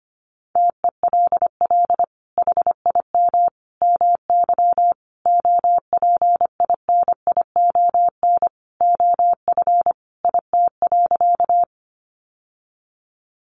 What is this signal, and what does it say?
Morse code, heard as a keyed tone, about 25 wpm, 700 Hz. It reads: TELL HIM MY OPINION OF IT.
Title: Sentences from Top 1200 Words 25wpm